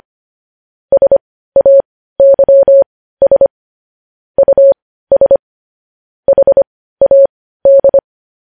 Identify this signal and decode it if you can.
Morse code, heard as a keyed tone, about 25 wpm, spaced as for 15 wpm, 550 Hz. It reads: SAYS US HAD
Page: news-headlines-in-morse-code-at-15-wpm